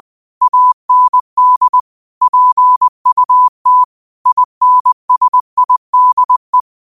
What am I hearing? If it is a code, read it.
Morse code, heard as a keyed tone, about 20 wpm, 1000 Hz. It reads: AND PUT INSIDE